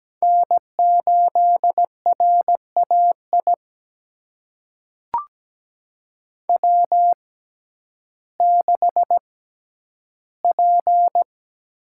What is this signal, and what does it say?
Morse code, heard as a keyed tone, about 17 wpm, spaced as for 10 wpm, 700 Hz. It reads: N8RAI W6P
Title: USA Call Signs - 5 Characters - 6x Character Spacing - 17wpm